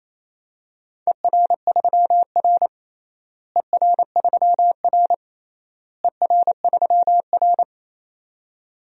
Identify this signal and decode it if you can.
Morse code, heard as a keyed tone, about 27 wpm, 700 Hz. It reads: ER3R ER3R ER3R
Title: DXCC-100 Call Signs - 4 Characters - Repeated 3x - 28wpm